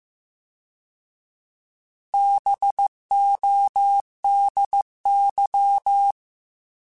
Morse code, heard as a keyed tone, about 15 wpm, 800 Hz. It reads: BODY